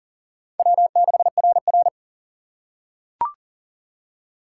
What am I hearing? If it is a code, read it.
Morse code, heard as a keyed tone, about 40 wpm, 700 Hz. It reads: W6RR
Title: USA Call Signs - 4 Characters - 6x Character Spacing - 40wpm